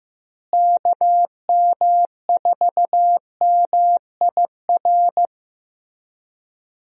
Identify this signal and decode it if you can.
Morse code, heard as a keyed tone, about 15 wpm, 700 Hz. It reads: KM4MIR
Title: USA Call Signs - 6 Characters - Repeated 2x - 15wpm